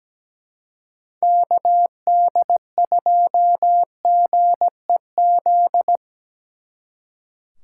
Morse code, heard as a keyed tone, about 20 wpm, 700 Hz. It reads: KD2GEZ
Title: Speed Racing - USA 2x3 Call Signs 17wpm